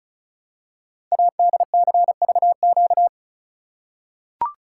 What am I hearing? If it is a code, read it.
Morse code, heard as a keyed tone, about 36 wpm, 700 Hz. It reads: ADCVQ
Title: Code Groups - 5 Letters - 35wpm